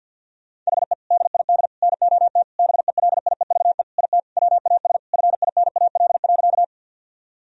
Morse code, heard as a keyed tone, about 52 wpm, 700 Hz. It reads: HE DID NOT BELIEVE IT WAS FINAL.
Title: Sentences from Top 1000 Words - Review Part 2 - 50wpm